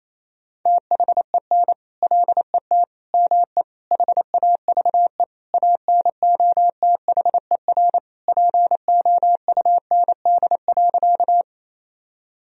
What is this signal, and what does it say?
Morse code, heard as a keyed tone, about 27 wpm, 700 Hz. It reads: THEN LET ME HAVE ANOTHER POUND.
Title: Sentences from Top 1000 Words - Review Part 2 - 28wpm